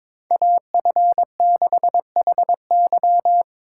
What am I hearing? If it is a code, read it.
Morse code, heard as a keyed tone, about 21 wpm, 700 Hz. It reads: AF6HY
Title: USA Call Signs - 5 Characters - 3x Character Spacing - 22wpm